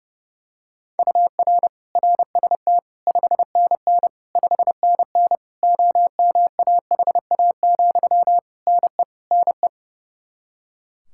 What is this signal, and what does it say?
Morse code, heard as a keyed tone, about 30 wpm, 700 Hz. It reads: UR RST 5NN 5NN OMAHA, NE NE
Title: Practice QSO - Exchange 1 - Beginning - 30wpm